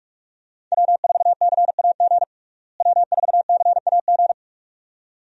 Morse code, heard as a keyed tone, about 46 wpm, 700 Hz. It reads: W4CAG W4CAG
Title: USA Call Signs - 5 Characters - Repeated 2x - 45wpm